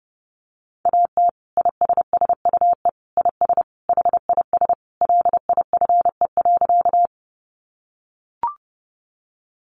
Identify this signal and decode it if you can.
Morse code, heard as a keyed tone, about 30 wpm, 700 Hz. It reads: AT ISSUE IS HIS LIFE.
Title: Sentences from Top 900 Words 30wpm